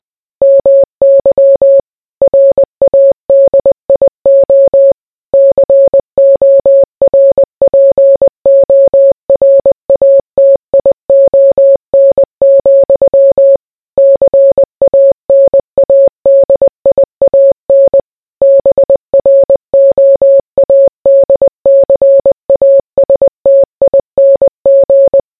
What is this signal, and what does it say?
Morse code, heard as a keyed tone, about 20 wpm, 550 Hz. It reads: MY RADIO CORPORATION, CANADIAN BROADCASTING